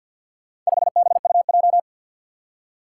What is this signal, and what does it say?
Morse code, heard as a keyed tone, about 51 wpm, 700 Hz. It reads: 5BUJ
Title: Four Challenging Letter-Number Combinations 50wpm